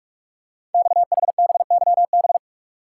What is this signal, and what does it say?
Morse code, heard as a keyed tone, about 44 wpm, 700 Hz. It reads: XHBYB
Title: Code Groups - 5 Letters - 45wpm